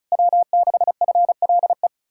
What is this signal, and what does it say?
Morse code, heard as a keyed tone, about 40 wpm, 700 Hz. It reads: W6FLE